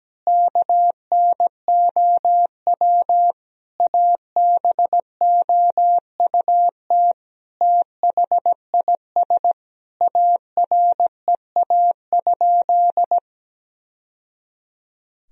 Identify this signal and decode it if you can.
Morse code, heard as a keyed tone, about 17 wpm, 700 Hz. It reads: KNOW ABOUT THIS AREA?
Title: Sentences from Top 1000 Words - Review Part 2 - 17wpm